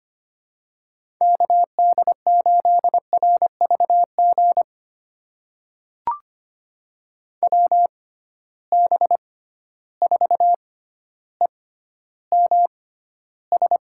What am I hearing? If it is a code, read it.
Morse code, heard as a keyed tone, about 25 wpm, spaced as for 13 wpm, 700 Hz. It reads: KD8RVG WB4EMS
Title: USA Call Signs - 6 Characters - 6x Character Spacing - 25wpm